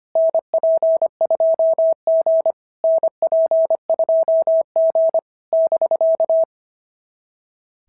Morse code, heard as a keyed tone, about 25 wpm, 690 Hz. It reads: NP2G NP2G <BK>
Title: SOTA QSO Practice 25wpm